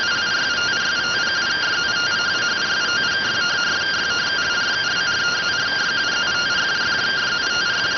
Two-tone teleprinter signal, about 170 Hz shift.